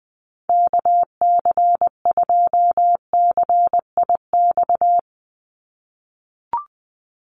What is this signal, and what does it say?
Morse code, heard as a keyed tone, about 20 wpm, 700 Hz. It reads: KC2CIX
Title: USA Call Signs 20wpm